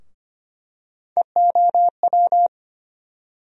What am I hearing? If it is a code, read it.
Morse code, heard as a keyed tone, about 25 wpm, 700 Hz. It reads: EOW
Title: USA Call Sign Postfixes - 3 Letters 25wpm